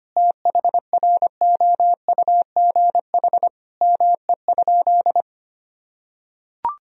Morse code, heard as a keyed tone, about 25 wpm, 700 Hz. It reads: THROUGH ME?
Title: Sentences from Top 1000 Words - Review Part 1 - 25wpm